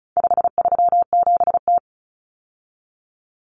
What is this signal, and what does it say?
Morse code, heard as a keyed tone, about 34 wpm, 700 Hz. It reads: 537T
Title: Common Cut Numbers - 4-Digit 35wpm